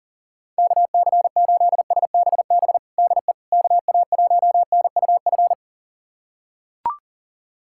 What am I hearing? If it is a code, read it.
Morse code, heard as a keyed tone, about 40 wpm, 700 Hz. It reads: KC8SBB DE KA1NUF
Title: USA Callsign DE Callsign - 6 Characters - 40wpm